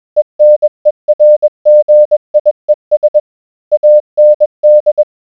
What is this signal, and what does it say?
Morse code, heard as a keyed tone, about 21 wpm, 600 Hz. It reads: ENERGIES AND